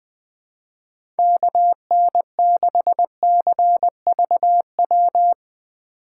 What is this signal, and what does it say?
Morse code, heard as a keyed tone, about 21 wpm, 700 Hz. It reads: KN6CVW
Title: USA Call Signs - 6 Characters - 3x Character Spacing - 20wpm